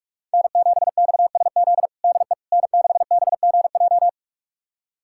Morse code, heard as a keyed tone, about 41 wpm, 700 Hz. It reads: N7XSZ DE N6BGJ
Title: USA Callsign DE Callsign - 5 Characters - 45wpm